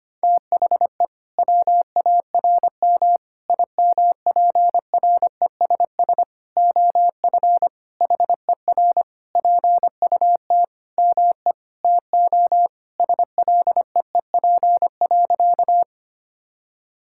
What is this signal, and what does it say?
Morse code, heard as a keyed tone, about 25 wpm, 700 Hz. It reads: THE WARM IMPRESS OF HER PUT ME TO SLEEP.
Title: Sentences from Top 1600 Words 25wpm